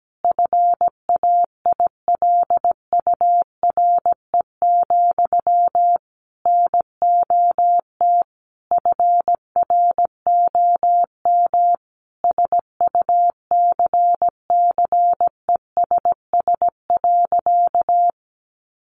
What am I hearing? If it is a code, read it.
Morse code, heard as a keyed tone, about 17 wpm, 700 Hz. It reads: FAILURE, NOT FROM SUCCESS.
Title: Sentences from Top 900 Words 17wpm